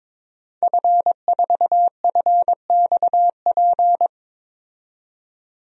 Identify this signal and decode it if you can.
Morse code, heard as a keyed tone, about 22 wpm, 700 Hz. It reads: F4FXP
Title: DXCC-100 Call Signs - 5 Characters - 22wpm